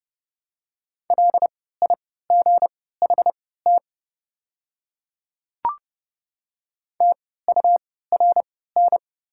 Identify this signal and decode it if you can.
Morse code, heard as a keyed tone, about 30 wpm, spaced as for 17 wpm, 700 Hz. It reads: LIGHT TURN
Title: 3x Character Spacing - Top 200 Words - 30wpm